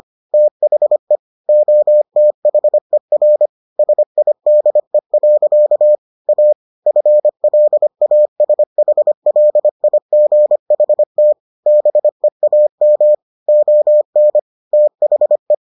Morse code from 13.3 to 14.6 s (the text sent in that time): ON